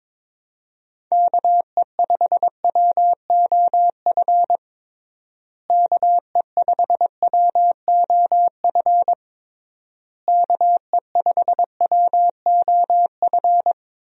Morse code, heard as a keyed tone, about 22 wpm, 700 Hz. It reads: KE5WOF KE5WOF KE5WOF
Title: USA Call Signs - 6 Characters - Repeated 3x - 22wpm